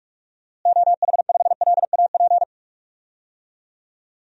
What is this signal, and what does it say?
Morse code, heard as a keyed tone, about 46 wpm, 700 Hz. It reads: OH5LAP
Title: DXCC-100 Call Signs - 6 Characters - Repeated 3x - 45wpm